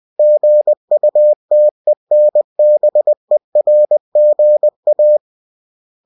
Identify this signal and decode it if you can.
Morse code, heard as a keyed tone, about 20 wpm, 600 Hz. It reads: GUTENBERGA
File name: cw-books / winnie-the-pooh / winnie-20wpm-page-0115